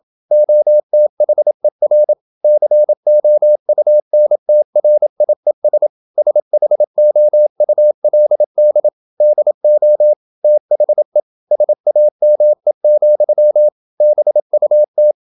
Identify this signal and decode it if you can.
Morse code, heard as a keyed tone, about 27 wpm, 600 Hz. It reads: OTHER COUNTRIES SHOULD DO THE SAME, BUT
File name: Never 0027wpm